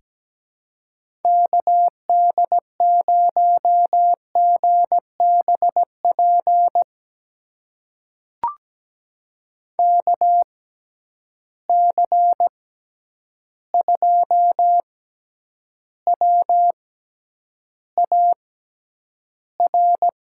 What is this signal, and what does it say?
Morse code, heard as a keyed tone, about 17 wpm, spaced as for 9 wpm, 700 Hz. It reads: KD0GBP KC2WAR
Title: USA Call Signs - 6 Characters - 6x Character Spacing - 17wpm